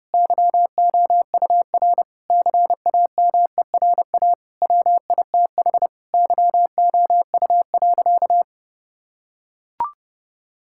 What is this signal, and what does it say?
Morse code, heard as a keyed tone, about 32 wpm, 700 Hz. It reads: YOUR CAMERA WITH YOU.
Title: Sentences from Top 1000 Words - Review Part 5 - 30wpm